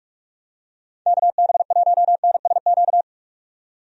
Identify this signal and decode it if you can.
Morse code, heard as a keyed tone, about 46 wpm, 700 Hz. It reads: KB1NSQ
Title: USA Call Signs - 6 Characters - 45wpm